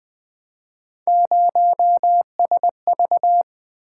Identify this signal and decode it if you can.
Morse code, heard as a keyed tone, about 20 wpm, 700 Hz. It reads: 0SV
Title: Three Letter-Number Combinations - 20wpm